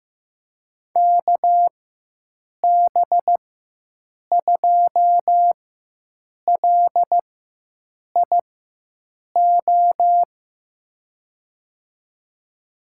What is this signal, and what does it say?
Morse code, heard as a keyed tone, about 15 wpm, spaced as for 7 wpm, 700 Hz. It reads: KB2LIO
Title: USA Call Signs - 6 Characters - 4x Character Spacing - 15wpm